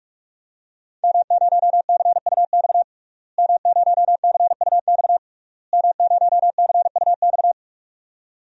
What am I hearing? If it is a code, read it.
Morse code, heard as a keyed tone, about 46 wpm, 700 Hz. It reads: M0CUX M0CUX M0CUX
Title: DXCC-100 Call Signs - 5 Characters - Repeated 3x - 45wpm